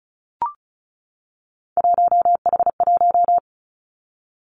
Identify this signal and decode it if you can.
Morse code, heard as a keyed tone, about 35 wpm, 700 Hz. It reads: 1H1
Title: Three Challenging Letter-Number Combinations 35wpm